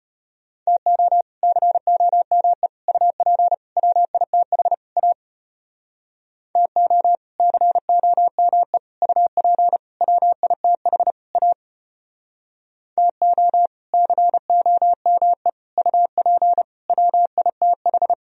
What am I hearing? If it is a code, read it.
Morse code, heard as a keyed tone, about 33 wpm, 700 Hz. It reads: TO COME UP WITH A TO COME UP WITH A TO COME UP WITH